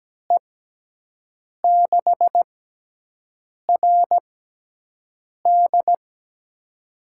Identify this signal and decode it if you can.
Morse code, heard as a keyed tone, about 17 wpm, spaced as for 6 wpm, 700 Hz. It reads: E6RD
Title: USA Call Signs - 5 Characters - 6x Character Spacing - 17wpm